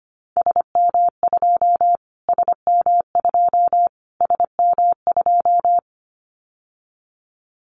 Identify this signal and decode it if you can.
Morse code, heard as a keyed tone, about 25 wpm, 700 Hz. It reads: SM2 SM2 SM2
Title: Callsign Prefixes for Top 25 DXCC - Repeated 3x - 25wpm